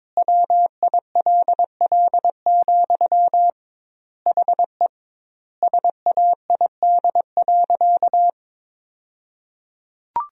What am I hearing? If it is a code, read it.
Morse code, heard as a keyed tone, about 22 wpm, 700 Hz. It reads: WILL, HE SAID.
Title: 2x Word-Spacing - Sentences from Top 200 Words - 22wpm